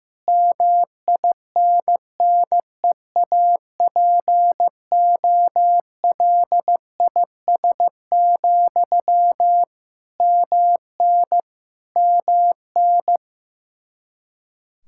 Morse code, heard as a keyed tone, about 15 wpm, 700 Hz. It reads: MINNEAPOLIS, MN MN
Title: Practice QSO - Exchange 1 - Beginning - 15wpm